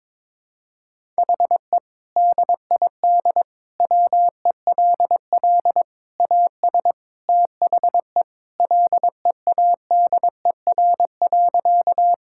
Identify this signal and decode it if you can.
Morse code, heard as a keyed tone, about 22 wpm, 700 Hz. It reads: HE DID WELL AS THE LEADER.
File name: practice / 22wpm / Sentences with 3-Word Phrases - 22wpm